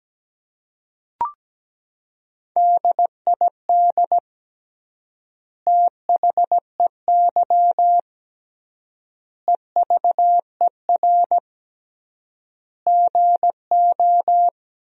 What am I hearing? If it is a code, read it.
Morse code, heard as a keyed tone, about 17 wpm, 700 Hz. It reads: DID THEY EVER GO
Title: 3x Word-Spacing - Sentences from Top 400 Words - 17wpm